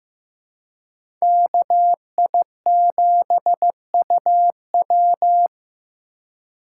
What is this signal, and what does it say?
Morse code, heard as a keyed tone, about 15 wpm, 700 Hz. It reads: KI7UW
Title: USA Call Signs - 2x Character Spacing - 15wpm